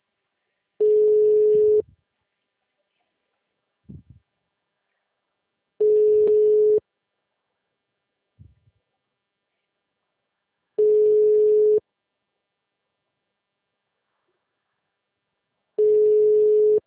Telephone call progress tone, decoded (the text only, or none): ringback tone